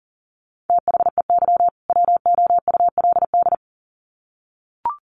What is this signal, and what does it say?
Morse code, heard as a keyed tone, about 40 wpm, 700 Hz. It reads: THEY WOULD
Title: Sets of 2 Words 40wpm